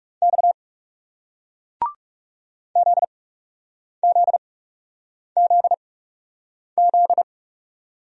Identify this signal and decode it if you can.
Morse code, heard as a keyed tone, about 39 wpm, spaced as for 8 wpm, 700 Hz. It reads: X ZZZZ